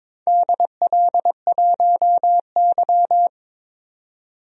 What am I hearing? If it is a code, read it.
Morse code, heard as a keyed tone, about 22 wpm, 700 Hz. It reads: DL1Y
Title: DXCC-100 Call Signs - 4 Characters - 22wpm